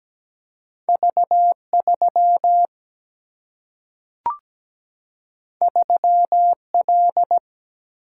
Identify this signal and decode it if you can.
Morse code, heard as a keyed tone, about 17 wpm, 700 Hz. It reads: V 3 3 L
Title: Single-Letter Single-Number 17wpm